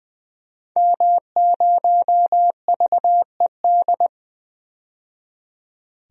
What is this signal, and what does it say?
Morse code, heard as a keyed tone, about 20 wpm, 700 Hz. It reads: M0VED
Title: DXCC-100 Call Signs - 5 Characters - 20wpm